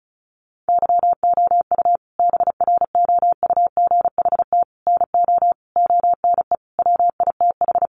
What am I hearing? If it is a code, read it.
Morse code, heard as a keyed tone, about 34 wpm, 700 Hz. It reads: YOU BROUGHT NO ONE WITH
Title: Sentences from Top 500 Words 35wpm